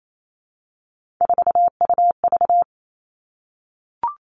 Morse code, heard as a keyed tone, about 28 wpm, 700 Hz. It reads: 4UV
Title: Three Challenging Letter-Number Combinations 28wpm